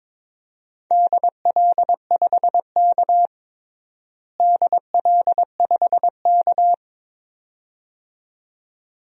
Morse code, heard as a keyed tone, about 22 wpm, 700 Hz. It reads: DL5K DL5K
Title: DXCC-100 Call Signs - 4 Characters - Repeated 2x - 22wpm